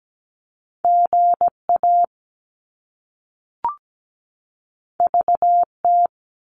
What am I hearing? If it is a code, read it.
Morse code, heard as a keyed tone, about 17 wpm, 700 Hz. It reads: GA VT
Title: US States 17wpm